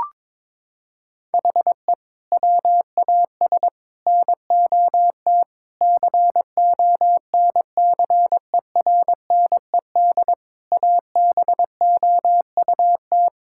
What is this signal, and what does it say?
Morse code, heard as a keyed tone, about 22 wpm, 700 Hz. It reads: HE WAS NOT CONCERNED ABOUT
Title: Sentences from Top 1000 Words - Review Part 6 - 22wpm